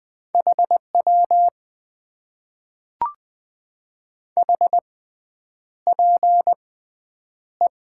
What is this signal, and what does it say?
Morse code, heard as a keyed tone, about 20 wpm, spaced as for 9 wpm, 700 Hz. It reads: HW HPE
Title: Top 90 QSO Elements - 6x Character Spacing - 20wpm